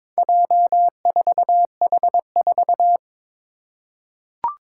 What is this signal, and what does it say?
Morse code, heard as a keyed tone, about 22 wpm, 700 Hz. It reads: J4H4
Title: Four Challenging Letter-Number Combinations 22wpm